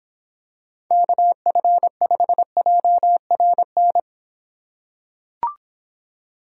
Morse code, heard as a keyed tone, about 26 wpm, 700 Hz. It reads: KF5JRN